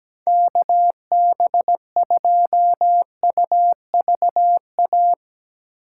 Morse code, heard as a keyed tone, about 17 wpm, 700 Hz. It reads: KB2UVA
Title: USA Call Signs - 8x Character Spacing - 17wpm